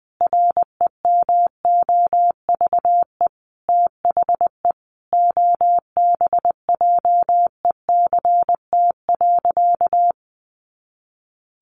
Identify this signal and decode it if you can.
Morse code, heard as a keyed tone, about 20 wpm, 700 Hz. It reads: REMOVE THE OBJECT.
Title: Sentences from Top 800 Words 20wpm